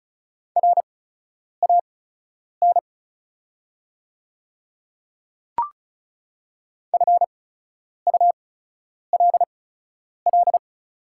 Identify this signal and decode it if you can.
Morse code, heard as a keyed tone, about 36 wpm, spaced as for 10 wpm, 690 Hz. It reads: RAN FULL
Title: 8x Character Spacing - Top 500 Words - 35wpm